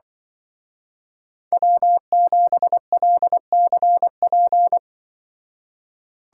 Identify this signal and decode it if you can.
Morse code, heard as a keyed tone, about 23 wpm, 700 Hz. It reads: W7LCP
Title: Speed Racing - USA 1x3 Call Signs 20wpm